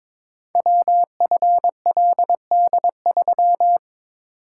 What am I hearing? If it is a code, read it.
Morse code, heard as a keyed tone, about 22 wpm, 700 Hz. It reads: WFLD3
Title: Code Groups - 5 Letter:Numbers - 22wpm